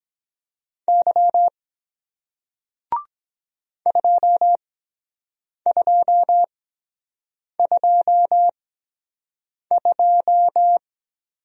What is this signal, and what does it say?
Morse code, heard as a keyed tone, about 23 wpm, spaced as for 7 wpm, 700 Hz. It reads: Y 2222